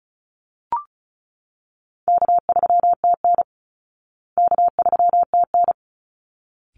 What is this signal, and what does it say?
Morse code, heard as a keyed tone, about 34 wpm, 700 Hz. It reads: K3TN K3TN
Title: Easy Call Signs Repeated 2x - 35wpm